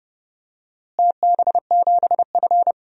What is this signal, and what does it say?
Morse code, heard as a keyed tone, about 30 wpm, 700 Hz. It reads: TB7F
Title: DXCC-100 Call Signs - 4 Characters - Repeated 2x - 30wpm